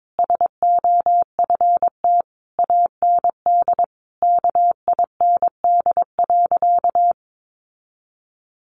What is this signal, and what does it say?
Morse code, heard as a keyed tone, about 22 wpm, 700 Hz. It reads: SOFT AND KIND.
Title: Sentences from Top 1500 Words 22wpm